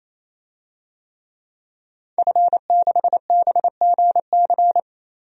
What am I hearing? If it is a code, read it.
Morse code, heard as a keyed tone, about 28 wpm, 700 Hz. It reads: F6BGC
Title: DXCC-100 Call Signs - 5 Characters - 28wpm